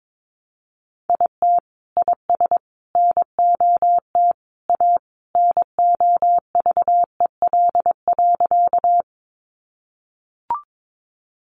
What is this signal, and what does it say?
Morse code, heard as a keyed tone, about 22 wpm, 700 Hz. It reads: IT IS NOT A NOVEL.
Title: Sentences from Top 1300 Words 22wpm